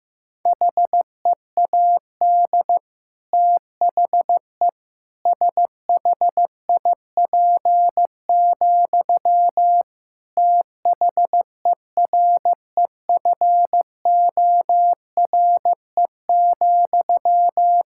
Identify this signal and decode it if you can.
Morse code, heard as a keyed tone, about 15 wpm, 700 Hz. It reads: HEAD THE SHIP, THEREFORE,
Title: Sentences from Top 1000 Words - Review Part 5 - 15wpm